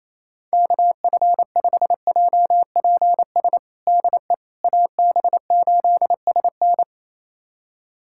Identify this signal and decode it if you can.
Morse code, heard as a keyed tone, about 28 wpm, 700 Hz. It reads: KF5JPS DE AB8SN
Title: USA Callsign DE Callsign 28wpm